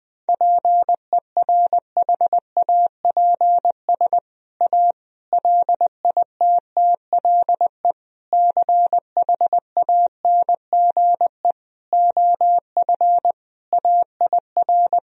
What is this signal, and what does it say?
Morse code, heard as a keyed tone, about 20 wpm, 700 Hz. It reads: PERHAPS A LITTLE CHANGE OF AIR
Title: Sentences from Top 1000 Words - Review Part 1 - 20wpm